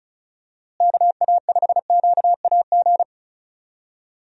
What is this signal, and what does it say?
Morse code, heard as a keyed tone, about 36 wpm, 700 Hz. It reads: KA5QAG
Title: USA Call Signs - 6 Characters - Repeated 3x - 35wpm